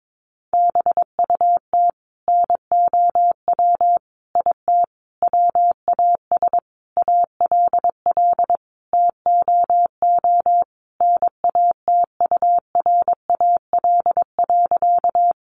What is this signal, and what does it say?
Morse code, heard as a keyed tone, about 22 wpm, 700 Hz. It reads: BUT NOW IT WAS ALL TOO NATURAL.
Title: Sentences from Top 700 Words 22wpm